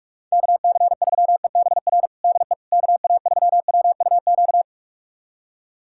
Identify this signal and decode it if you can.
Morse code, heard as a keyed tone, about 44 wpm, 700 Hz. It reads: KC3EBR DE KA3WUQ
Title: USA Callsign DE Callsign - 6 Characters - 45wpm